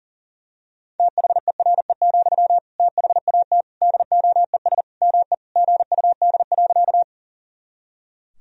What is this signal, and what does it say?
Morse code, heard as a keyed tone, about 40 wpm, 700 Hz. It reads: THERE, THAT DOES ME GUD.
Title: Sentences from Top 200 Words and Abbreviations 40wpm